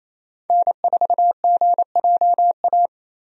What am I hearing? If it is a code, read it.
Morse code, heard as a keyed tone, about 28 wpm, 700 Hz. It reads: N4GJA
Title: USA Call Signs - 5 Characters - 28wpm